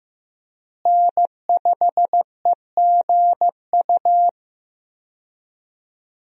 Morse code, heard as a keyed tone, about 15 wpm, 700 Hz. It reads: N5EGU
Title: USA Call Signs - 5 Characters - Repeated 3x - 15wpm